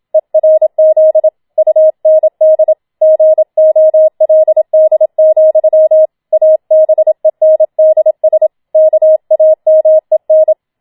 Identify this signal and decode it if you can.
Morse code, heard as a keyed tone, about 27 wpm, 630 Hz. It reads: ERZ UND GOLD, ABENDS KAMEN